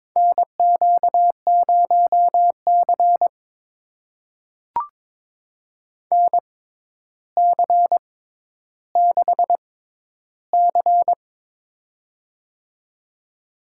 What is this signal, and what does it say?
Morse code, heard as a keyed tone, about 22 wpm, spaced as for 11 wpm, 700 Hz. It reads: NQ0C NC6C
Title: USA Call Signs - 4 Characters - 6x Character Spacing - 22wpm